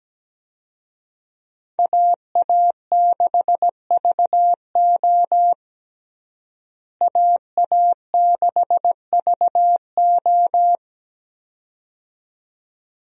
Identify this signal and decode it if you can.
Morse code, heard as a keyed tone, about 17 wpm, 700 Hz. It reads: AA6VO AA6VO
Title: USA Call Signs - 5 Characters - Repeated 2x - 17wpm